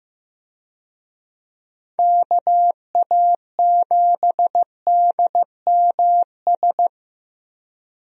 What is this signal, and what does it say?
Morse code, heard as a keyed tone, about 15 wpm, 700 Hz. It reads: KA7DMS
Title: USA Call Signs - 6 Characters - 15wpm